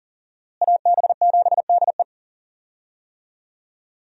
Morse code, heard as a keyed tone, about 40 wpm, 700 Hz. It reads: AB7DE
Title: USA Call Signs - 5 Characters - Repeated 3x - 40wpm